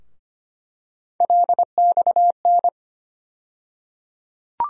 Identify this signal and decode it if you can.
Morse code, heard as a keyed tone, about 25 wpm, 700 Hz. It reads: LXN
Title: USA Call Sign Postfixes - 3 Letters 25wpm